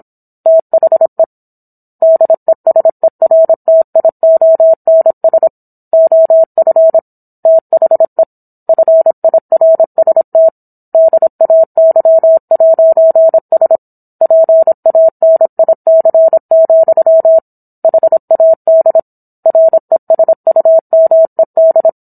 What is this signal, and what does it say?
Morse code, heard as a keyed tone, about 27 wpm, 650 Hz. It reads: THE DESERTIONS OF THE FIRST DAY'S PANIC, HAD RESUMED